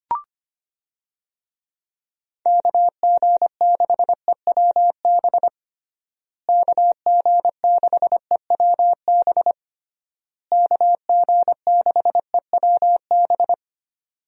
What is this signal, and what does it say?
Morse code, heard as a keyed tone, about 25 wpm, 700 Hz. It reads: KG6EWB KG6EWB KG6EWB
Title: USA Call Signs - 6 Characters - Repeated 3x - 25wpm